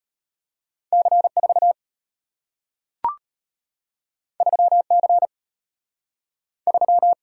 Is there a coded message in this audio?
Yes: Morse code from 0.9 to 7.1 s, heard as a keyed tone, about 34 wpm, 700 Hz. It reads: C4 3C 3